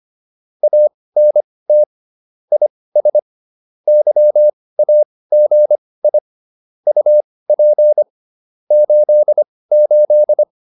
Morse code, heard as a keyed tone, about 25 wpm, spaced as for 18 wpm, 600 Hz. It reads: ANT IS YAGI UP 88
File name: QSO303 18